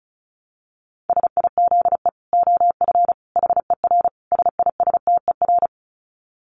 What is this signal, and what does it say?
Morse code, heard as a keyed tone, about 34 wpm, 700 Hz. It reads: SIZE OF HER SISTER
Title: Dit Challenge 35wpm